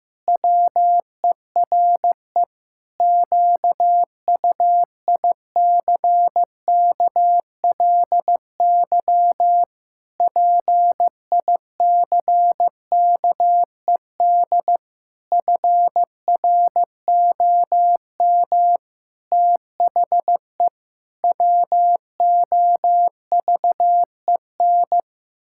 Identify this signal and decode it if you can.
Morse code, heard as a keyed tone, about 15 wpm, 700 Hz. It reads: WERE QUICKLY PICKED FROM THE WOVEN
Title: Pangram Challenge v2 - 15wpm